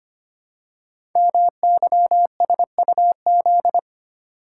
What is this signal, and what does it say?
Morse code, heard as a keyed tone, about 25 wpm, 700 Hz. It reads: MYSUZ
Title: Code Groups - 5 Letters - 25wpm